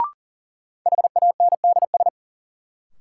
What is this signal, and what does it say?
Morse code, heard as a keyed tone, about 40 wpm, 700 Hz. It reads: HANDS
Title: Sets of 4 Words - New Word Drill - 40wpm